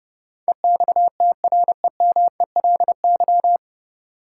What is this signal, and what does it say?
Morse code, heard as a keyed tone, about 30 wpm, 700 Hz. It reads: EXTREMELY
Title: Sentences from Top 1000 Words - Review Part 7 - New Word Drill - 30wpm